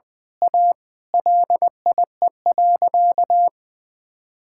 Morse code, heard as a keyed tone, about 20 wpm, 700 Hz. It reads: A LIE.
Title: Sentences from Top 1600 Words 20wpm